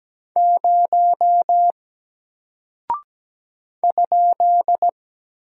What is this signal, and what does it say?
Morse code, heard as a keyed tone, about 17 wpm, 700 Hz. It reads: 0 ?
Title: Single Character (Letter:Number:Punctuation:Prosign) 17wpm